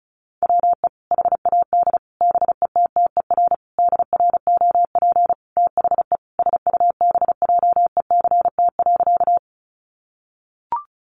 Morse code, heard as a keyed tone, about 34 wpm, 700 Hz. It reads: WE HAD BETTER DROP THE SUBJECT.
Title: Sentences from Top 1400 Words 35wpm